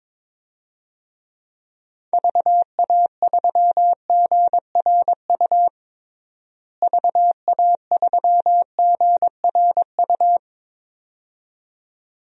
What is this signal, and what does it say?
Morse code, heard as a keyed tone, about 21 wpm, 700 Hz. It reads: VA3GRU VA3GRU
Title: DXCC-100 Call Signs - 6 Characters - Repeated 2x - 22wpm